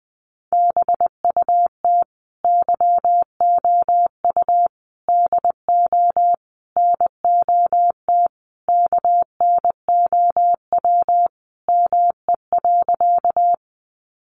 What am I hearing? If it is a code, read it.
Morse code, heard as a keyed tone, about 20 wpm, 700 Hz. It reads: BUT YOU DO NOT KNOW ME.
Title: Sentences from Top 600 Words 20wpm